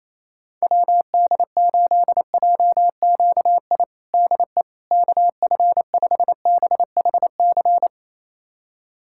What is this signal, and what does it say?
Morse code, heard as a keyed tone, about 28 wpm, 700 Hz. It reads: WD8JQI DE KF5BHC
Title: USA Callsign DE Callsign 28wpm